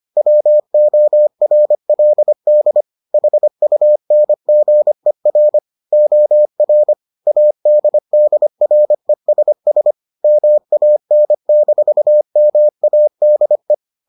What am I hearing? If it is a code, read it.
Morse code, heard as a keyed tone, about 25 wpm, 600 Hz. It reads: WORLD HUNGER OR ADDRESS MAN-MADE